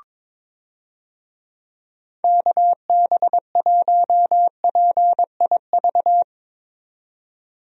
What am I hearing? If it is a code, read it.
Morse code, heard as a keyed tone, about 22 wpm, 700 Hz. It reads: KB1PIV
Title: USA Call Signs - 6 Characters - 22wpm